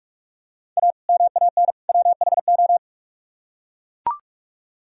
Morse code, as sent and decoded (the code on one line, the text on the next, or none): .- / -- .- -. / .-- .... ---
A MAN WHO